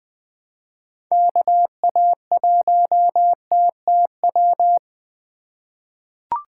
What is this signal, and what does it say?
Morse code, heard as a keyed tone, about 20 wpm, 700 Hz. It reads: KA1TTW
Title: USA Call Signs - 6 Characters - 3x Character Spacing - 20wpm